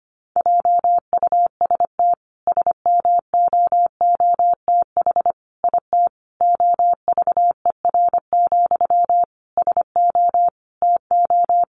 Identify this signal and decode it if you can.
Morse code, heard as a keyed tone, about 25 wpm, 700 Hz. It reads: JUST SMOOTH IT OVER, SO TO
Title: Sentences from Top 1400 Words 25wpm